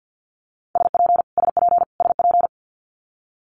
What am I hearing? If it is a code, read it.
Morse code, heard as a keyed tone, about 50 wpm, 700 Hz. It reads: SP SP SP
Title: Multi-letter Phonemes Repeated 3x 50wpm